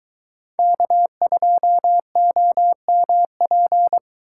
Morse code, heard as a keyed tone, about 22 wpm, 700 Hz. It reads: K2OMP